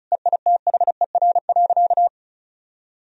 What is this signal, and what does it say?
Morse code, heard as a keyed tone, about 34 wpm, 700 Hz. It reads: EITHER.
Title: Sentences from Top 1000 Words - Review Part 4 - 35wpm